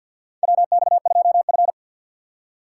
Morse code, heard as a keyed tone, about 52 wpm, 700 Hz. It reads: WX2F
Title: USA Call Signs - 4 Characters - 2x Character Spacing - 50wpm